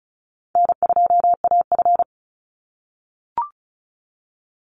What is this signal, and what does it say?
Morse code, heard as a keyed tone, about 34 wpm, 700 Hz. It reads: N2AF
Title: NA SOTA Chaser Callsigns 35wpm